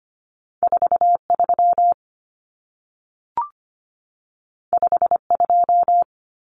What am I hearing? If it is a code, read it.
Morse code, heard as a keyed tone, about 25 wpm, 700 Hz. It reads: 43 52
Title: Numbers - 2-Digit 25wpm